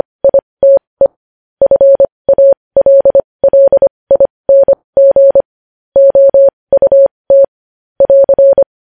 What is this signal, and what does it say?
Morse code, heard as a keyed tone, about 25 wpm, spaced as for 20 wpm, 550 Hz. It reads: ITE FALLING OUT <AR>
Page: News Headlines in Morse Code at 20 WPM